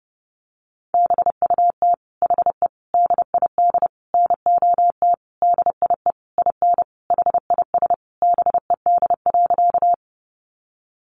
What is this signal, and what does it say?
Morse code, heard as a keyed tone, about 30 wpm, 700 Hz. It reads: BUT HE DID NOT DIE IN HIS BED.
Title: Sentences from Top 1300 Words 30wpm